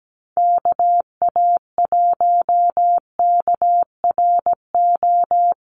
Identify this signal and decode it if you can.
Morse code, heard as a keyed tone, about 17 wpm, 700 Hz. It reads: KA1KRO
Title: USA Call Signs Repeated 2x 17wpm